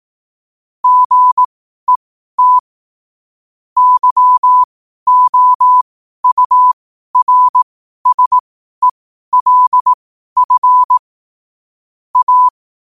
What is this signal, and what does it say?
Morse code, heard as a keyed tone, about 18 wpm, spaced as for 12 wpm, 1000 Hz. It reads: GET YOURSELF A
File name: qotdfiles / 12wpm